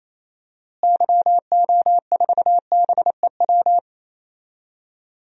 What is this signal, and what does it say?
Morse code, heard as a keyed tone, about 28 wpm, 700 Hz. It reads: YO4BEW
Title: DXCC-100 Call Signs - 6 Characters - Repeated 3x - 28wpm